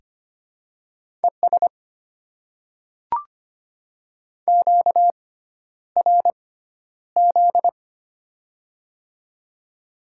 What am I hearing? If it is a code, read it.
Morse code, heard as a keyed tone, about 25 wpm, spaced as for 11 wpm, 700 Hz. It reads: ES QRZ
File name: Top 90 QSO Elements - 6x Character Spacing - 25wpm